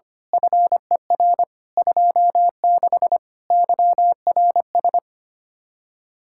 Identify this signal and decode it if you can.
Morse code, heard as a keyed tone, about 24 wpm, 700 Hz. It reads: FER 26 YRS
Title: Speed-Racing Common QSO Phrases Encore - 25wpm